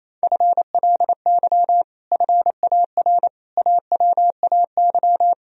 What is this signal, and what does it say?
Morse code, heard as a keyed tone, about 27 wpm, 700 Hz. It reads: FLY FAR AWAY
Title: Sentences from Top 1000 Words - Review Part 2 - 28wpm